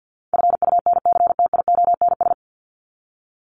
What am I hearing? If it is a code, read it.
Morse code, heard as a keyed tone, about 50 wpm, 700 Hz. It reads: FUNCTIONS
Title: ES:S Suffix - Top 100 Words with Base Words 50wpm